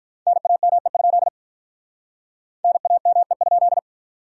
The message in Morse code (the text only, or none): NAME? NAME?